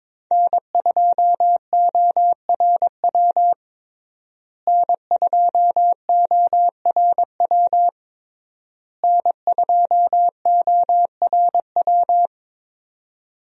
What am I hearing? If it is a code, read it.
Morse code, heard as a keyed tone, about 22 wpm, 700 Hz. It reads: N2ORW N2ORW N2ORW
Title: USA Call Signs - 5 Characters - Repeated 3x - 22wpm